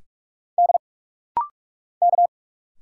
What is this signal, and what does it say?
Morse code, heard as a keyed tone, about 46 wpm, 690 Hz. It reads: D K
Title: Single Letters - Rapid-Fire 45wpm